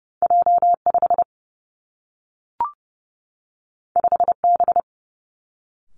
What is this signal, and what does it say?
Morse code, heard as a keyed tone, about 30 wpm, 700 Hz. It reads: J 5 5 B
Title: Two Challenging Letter-Number Combinations 30wpm